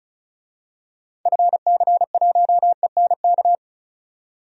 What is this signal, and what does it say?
Morse code, heard as a keyed tone, about 36 wpm, 700 Hz. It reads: FC1ENK
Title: DXCC-100 Call Signs - 6 Characters - 35wpm